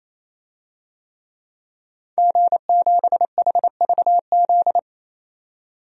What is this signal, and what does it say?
Morse code, heard as a keyed tone, about 28 wpm, 700 Hz. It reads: G7HVZ
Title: DXCC-100 Call Signs - 5 Characters - 28wpm